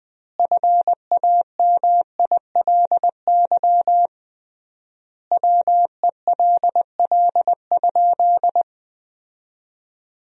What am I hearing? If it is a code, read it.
Morse code, heard as a keyed tone, about 20 wpm, 700 Hz. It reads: FAMILY WELL?
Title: 3x Word-Spacing - Sentences from Top 400 Words - 20wpm